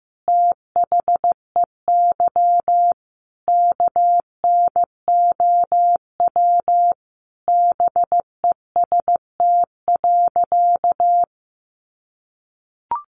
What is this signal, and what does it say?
Morse code, heard as a keyed tone, about 15 wpm, 700 Hz. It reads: THEY KNOW BEST.
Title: Sentences from Top 600 Words 15wpm